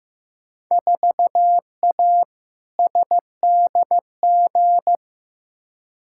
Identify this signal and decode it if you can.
Morse code, heard as a keyed tone, about 15 wpm, 700 Hz. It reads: 4A SDG
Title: Field Day Exchanges - 15wpm